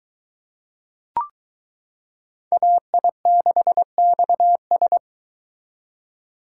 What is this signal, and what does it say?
Morse code, heard as a keyed tone, about 22 wpm, 700 Hz. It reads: AI6XS